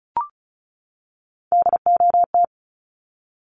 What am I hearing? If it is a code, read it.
Morse code, heard as a keyed tone, about 34 wpm, 700 Hz. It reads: DOT
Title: Top 1251-1500 Words 35wpm